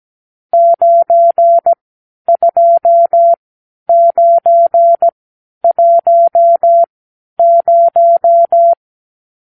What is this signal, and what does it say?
Morse code, heard as a keyed tone, about 17 wpm, spaced as for 10 wpm, 680 Hz. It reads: 92910
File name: arrl / Review (1-0)